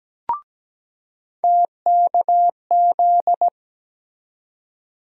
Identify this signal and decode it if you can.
Morse code, heard as a keyed tone, about 17 wpm, 700 Hz. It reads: TKZ
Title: USA Call Sign Postfixes - 3 Letters 17wpm